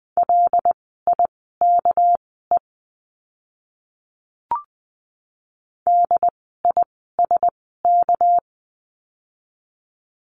Morse code, heard as a keyed tone, about 20 wpm, spaced as for 14 wpm, 700 Hz. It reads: LIKE DISK
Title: 199 Lesson 32f - Letter k - Words 20wpm